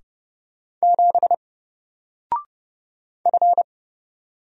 Morse code, heard as a keyed tone, about 30 wpm, 700 Hz. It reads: 7 F
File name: Single-Letter-Number-30wpm